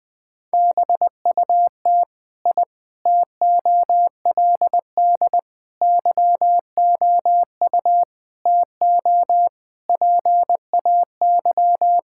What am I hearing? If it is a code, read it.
Morse code, heard as a keyed tone, about 20 wpm, 700 Hz. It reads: BUT I TOLD YOU TO PAY
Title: Sentences from Top 1000 Words - Review Part 6 - 20wpm